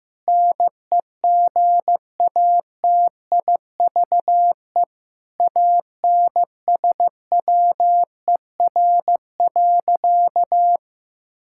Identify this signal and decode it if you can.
Morse code, heard as a keyed tone, about 15 wpm, 700 Hz. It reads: NEGATIVE ANSWER.
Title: Sentences from Top 1000 Words - Review Part 6 - 15wpm